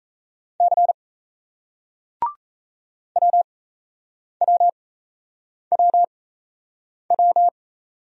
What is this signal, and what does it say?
Morse code, heard as a keyed tone, about 35 wpm, spaced as for 8 wpm, 700 Hz. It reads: C WWWW
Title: Speed-Racing Single Letters 28wpm